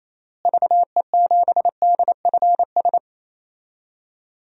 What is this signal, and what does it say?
Morse code, heard as a keyed tone, about 28 wpm, 700 Hz. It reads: VE7DFS
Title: DXCC-100 Call Signs - 6 Characters - Repeated 3x - 28wpm